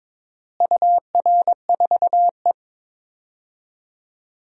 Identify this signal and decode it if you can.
Morse code, heard as a keyed tone, about 22 wpm, 700 Hz. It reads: UR4E
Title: DXCC-100 Call Signs - 4 Characters - Repeated 2x - 22wpm